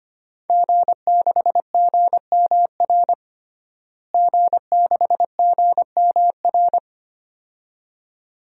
Morse code, heard as a keyed tone, about 25 wpm, 700 Hz. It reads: G6GMR G6GMR
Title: DXCC-100 Call Signs - 5 Characters - Repeated 2x - 25wpm